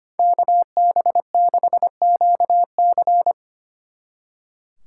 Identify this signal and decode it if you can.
Morse code, heard as a keyed tone, about 28 wpm, 700 Hz. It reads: KB6QC
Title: Speed Racing - USA 2x2 Call Signs 25wpm